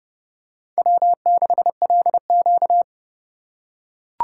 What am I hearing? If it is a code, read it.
Morse code, heard as a keyed tone, about 30 wpm, 700 Hz. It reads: W6LQ
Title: USA Call Signs - 4 Characters - 6x Character Spacing - 30wpm